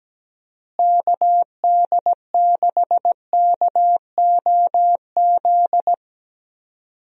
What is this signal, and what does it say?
Morse code, heard as a keyed tone, about 17 wpm, 700 Hz. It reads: KD6KOZ
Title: USA Call Signs - 6 Characters - Repeated 2x - 17wpm